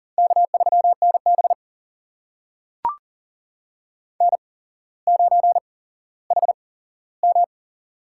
Morse code, heard as a keyed tone, about 40 wpm, spaced as for 17 wpm, 700 Hz. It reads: K3NB N9HM
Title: USA Call Signs - 4 Characters - 8x Character Spacing - 40wpm